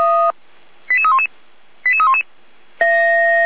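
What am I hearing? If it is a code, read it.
Five-tone selective call (ZVEI), heard as quick runs of five tones, tones of about 80 ms. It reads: 88311 88311
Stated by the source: ZVEI five-tone call 88335 received by SDR, from the feuerwehr-orlamuende (fire brigade Orlamünde) server.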